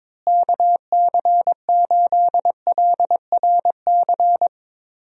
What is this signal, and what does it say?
Morse code, heard as a keyed tone, about 22 wpm, 700 Hz. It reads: KC8LRC
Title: USA Call Signs - 6 Characters - Repeated 3x - 22wpm